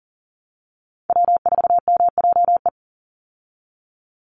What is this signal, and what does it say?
Morse code, heard as a keyed tone, about 37 wpm, 700 Hz. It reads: W4MJE
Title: Call Signs Repeated 2x - 40wpm